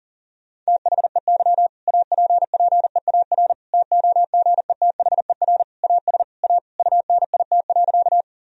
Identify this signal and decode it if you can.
Morse code, heard as a keyed tone, about 37 wpm, 700 Hz. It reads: THEY APPEAR TOGETHER AS A UNIT.
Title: Sentences from Top 1000 Words - Review Part 2 - 40wpm